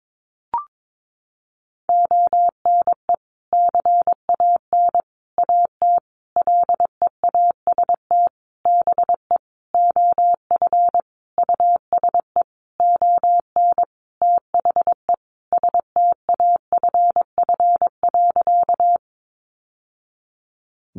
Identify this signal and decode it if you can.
Morse code, heard as a keyed tone, about 22 wpm, 700 Hz. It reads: ONE CAN AT LEAST BE OF USE ON THE STAFF.
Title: Sentences from Top 1200 Words 22wpm